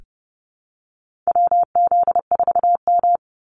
Morse code, heard as a keyed tone, about 30 wpm, 700 Hz. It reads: WZ4M
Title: NA SOTA Activator Callsigns 30wpm